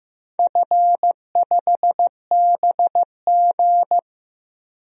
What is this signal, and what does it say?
Morse code, heard as a keyed tone, about 15 wpm, 700 Hz. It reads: F5BG
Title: DXCC-100 Call Signs - 4 Characters - 15wpm